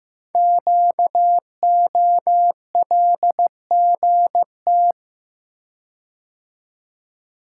Morse code, heard as a keyed tone, about 15 wpm, 700 Hz. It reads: QOLGT
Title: Code Groups - 5 Letters - 15wpm